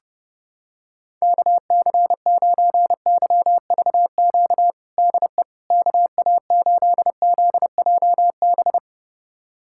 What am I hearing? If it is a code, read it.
Morse code, heard as a keyed tone, about 31 wpm, 700 Hz. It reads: KC9YVQ DE KA8ZJB
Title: USA Callsign DE Callsign - 6 Characters - 30wpm